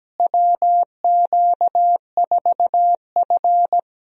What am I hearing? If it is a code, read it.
Morse code, heard as a keyed tone, about 17 wpm, 700 Hz. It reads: WQ4F
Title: USA Call Signs - 4 Characters - Repeated 2x - 17wpm